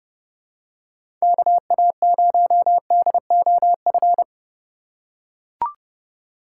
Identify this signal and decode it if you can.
Morse code, heard as a keyed tone, about 30 wpm, 700 Hz. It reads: KA0DOF
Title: USA Call Signs - 6 Characters - 2x Character Spacing - 30wpm